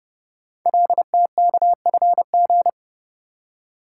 Morse code, heard as a keyed tone, about 28 wpm, 700 Hz. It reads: LTKFG
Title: Code Groups - 5 Letters - 30wpm